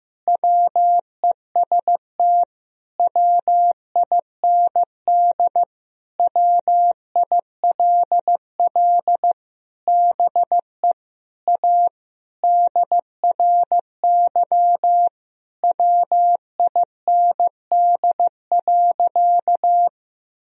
Morse code, heard as a keyed tone, about 15 wpm, 700 Hz. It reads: WEST WIND WILL BE A DRY WIND.
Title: Sentences from Top 1000 Words - Review Part 3 - 15wpm